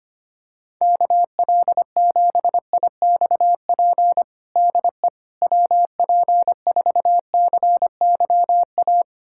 Morse code, heard as a keyed tone, about 25 wpm, 700 Hz. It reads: KL7IXP DE WP4CYA
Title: USA Callsign DE Callsign - 6 Characters - 25wpm